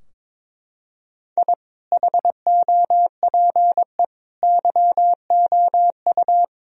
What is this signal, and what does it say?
Morse code, heard as a keyed tone, about 22 wpm, 700 Hz. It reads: I HOPE YOU
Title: Sentences from Top 1000 Words - Review Part 2 - 22wpm